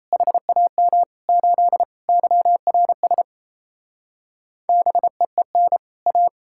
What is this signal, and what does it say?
Morse code, heard as a keyed tone, about 37 wpm, 700 Hz. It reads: HAM 8 YRS BEEN A